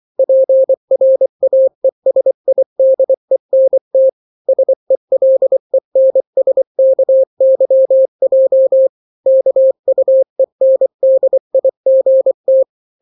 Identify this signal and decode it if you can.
Morse code, heard as a keyed tone, about 24 wpm, 530 Hz. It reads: PRAESIDENT SELENSKYJ KUENDIGT
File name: daily / 23wpm